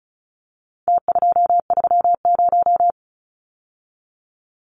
Morse code, heard as a keyed tone, about 34 wpm, 700 Hz. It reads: T230
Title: Numbers - Contest Style - 4-Digit 35wpm